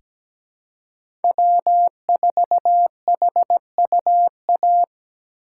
Morse code, heard as a keyed tone, about 17 wpm, 700 Hz. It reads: W4HUA
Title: USA Call Signs - 5 Characters - 3x Character Spacing - 17wpm